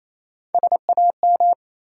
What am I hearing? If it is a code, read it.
Morse code, heard as a keyed tone, about 28 wpm, 700 Hz. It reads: SAM
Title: USA Common Names - Max 4 Letters - 28wpm